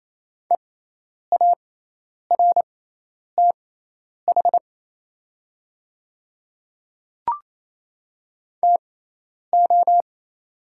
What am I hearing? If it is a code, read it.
Morse code, heard as a keyed tone, about 27 wpm, spaced as for 10 wpm, 700 Hz. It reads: EARTH TO
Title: 6x Character Spacing - Top 200 Words - 28wpm